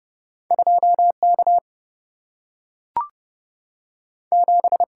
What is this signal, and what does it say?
Morse code, heard as a keyed tone, about 30 wpm, 700 Hz. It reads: 2 K 7
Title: Single-Letter Single-Number 30wpm